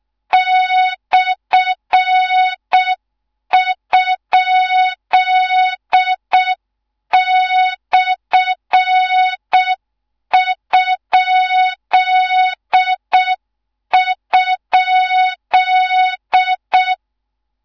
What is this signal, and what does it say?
Morse code, heard as a keyed tone, about 6 wpm, 3700 Hz. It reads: /?/??
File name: Morze-HA2MN / hang / 13IRASJELEK1 HA2MN